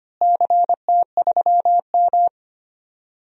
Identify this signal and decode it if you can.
Morse code, heard as a keyed tone, about 25 wpm, 700 Hz. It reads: CT3M
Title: DXCC-100 Call Signs - 4 Characters - Repeated 2x - 25wpm